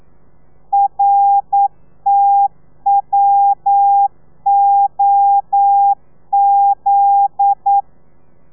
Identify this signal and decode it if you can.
Morse code, heard as a keyed tone, about 9 wpm, 800 Hz. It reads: RTWOZ
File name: telegrafia / i2tfj / Lez9 40